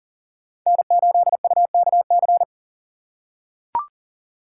Morse code, heard as a keyed tone, about 37 wpm, 700 Hz. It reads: N8UKC
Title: USA Call Signs - 5 Characters - 2x Character Spacing - 40wpm